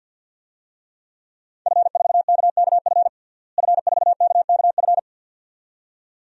Morse code, heard as a keyed tone, about 52 wpm, 700 Hz. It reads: F4KKF F4KKF
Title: DXCC-100 Call Signs - 5 Characters - Repeated 2x - 50wpm